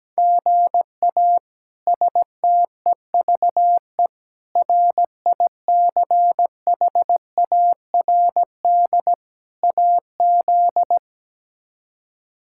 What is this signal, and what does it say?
Morse code, heard as a keyed tone, about 17 wpm, 700 Hz. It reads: GA STEVE RICHARD AZ
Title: K1USN SST QSO Practice - 17wpm